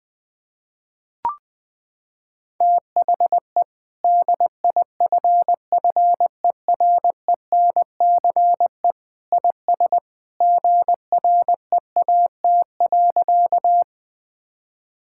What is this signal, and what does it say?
Morse code, heard as a keyed tone, about 20 wpm, 700 Hz. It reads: THE DIFFERENCE IS GREAT.
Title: Sentences from Top 1000 Words - Review Part 5 - 20wpm